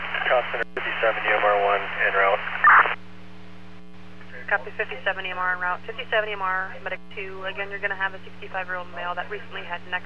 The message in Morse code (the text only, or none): none